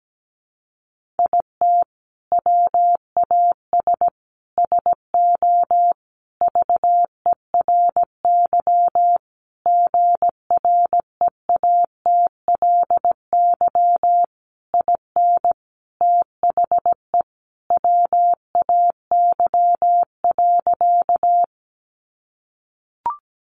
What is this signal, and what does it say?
Morse code, heard as a keyed tone, about 17 wpm, 700 Hz. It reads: IT WAS SO VERY GREATLY IN THE WAY.
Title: Sentences from Top 1400 Words 17wpm